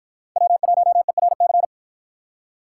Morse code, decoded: W1ERC

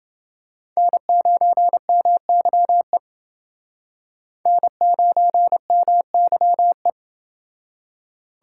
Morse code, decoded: N9MYE N9MYE

